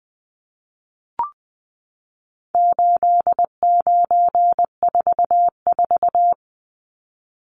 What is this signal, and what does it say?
Morse code, heard as a keyed tone, about 20 wpm, 700 Hz. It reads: 8944
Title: Numbers - 4-Digit 20wpm